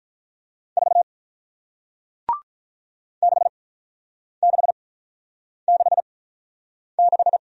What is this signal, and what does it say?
Morse code, heard as a keyed tone, about 48 wpm, spaced as for 9 wpm, 700 Hz. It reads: 4 6666